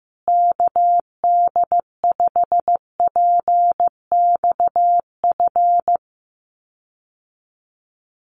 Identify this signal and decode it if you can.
Morse code, heard as a keyed tone, about 15 wpm, 700 Hz. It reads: KD5PXF
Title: USA Call Signs Repeated 3x 15wpm